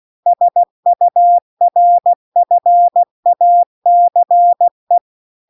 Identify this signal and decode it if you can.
Morse code, heard as a keyed tone, about 16 wpm, 700 Hz. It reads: SURFACE
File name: CW-The-Big-Twoheaded-River-1-16WPM